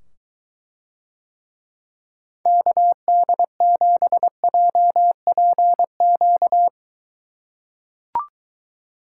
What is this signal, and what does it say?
Morse code, heard as a keyed tone, about 23 wpm, 700 Hz. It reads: KD7JPQ